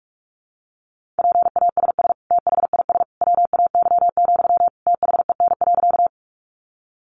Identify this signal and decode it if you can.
Morse code, heard as a keyed tone, about 44 wpm, 700 Hz. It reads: PASS THIS WAY, THEN.
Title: Sentences from Top 400 Words 45wpm